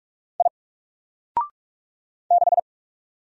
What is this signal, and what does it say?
Morse code, heard as a keyed tone, about 45 wpm, 700 Hz. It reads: I 6
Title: Single Letter-Number 45wpm